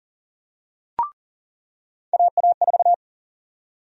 Morse code, heard as a keyed tone, about 40 wpm, 700 Hz. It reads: AA4
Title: USA Call Sign Prefixes - 40wpm